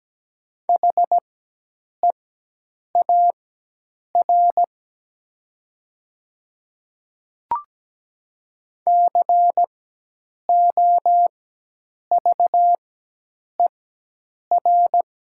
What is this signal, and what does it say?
Morse code, heard as a keyed tone, about 17 wpm, spaced as for 8 wpm, 700 Hz. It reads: HEAR COVER
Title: 4x Character Spacing - Top 500 Words - 17wpm